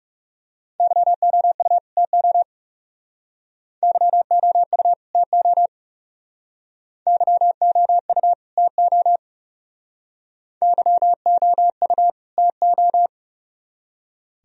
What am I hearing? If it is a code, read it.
Morse code, heard as a keyed tone, about 36 wpm, 700 Hz. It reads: YOU TO YOU TO YOU TO YOU TO
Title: Speed-Racing - Sets of 2 Words 30wpm